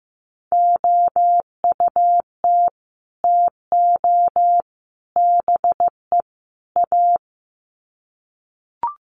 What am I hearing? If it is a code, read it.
Morse code, heard as a keyed tone, about 15 wpm, 700 Hz. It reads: OUT TO BE A
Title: Sets of 4 Words 15wpm